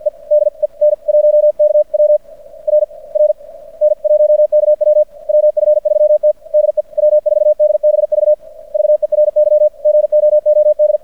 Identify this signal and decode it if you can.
Morse code, heard as a keyed tone, about 55 wpm, spaced as for 46 wpm, 600 Hz. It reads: EDEN 1KW RR N 1KW WF3T DE W4DBV VERY GOOD
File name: archived / CRSnet